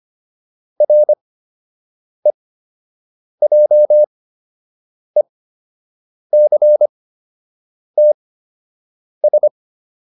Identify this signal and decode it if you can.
Morse code, heard as a keyed tone, about 25 wpm, spaced as for 7 wpm, 600 Hz. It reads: REJECTS